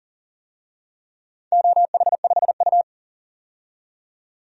Morse code, heard as a keyed tone, about 39 wpm, 700 Hz. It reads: OH5U
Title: DXCC-100 Call Signs - 4 Characters - 40wpm